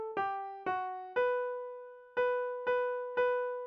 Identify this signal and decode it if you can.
There is no code.